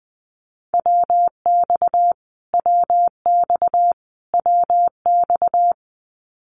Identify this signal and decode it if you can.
Morse code, heard as a keyed tone, about 20 wpm, 700 Hz. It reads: WX WX WX
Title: Abbreviations Repeated 3x - 20wpm